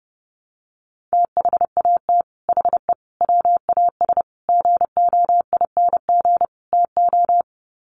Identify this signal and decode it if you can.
Morse code, heard as a keyed tone, about 30 wpm, 700 Hz. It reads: THAT HE WAS GOING TO
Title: Sets of 5 Words 30wpm